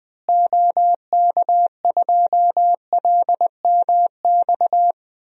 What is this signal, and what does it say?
Morse code, heard as a keyed tone, about 20 wpm, 700 Hz. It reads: OK2LMX
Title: DXCC-100 Call Signs - 6 Characters - Repeated 3x - 20wpm